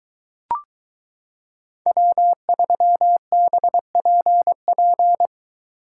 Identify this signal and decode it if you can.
Morse code, heard as a keyed tone, about 22 wpm, 700 Hz. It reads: W3BPP